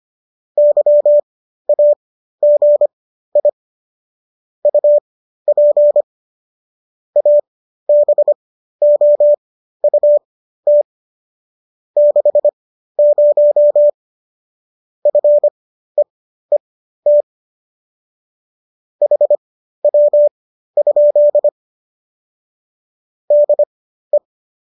Morse code, heard as a keyed tone, about 25 wpm, spaced as for 13 wpm, 590 Hz. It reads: YAGI UP ABOUT 60 FEET HW? DE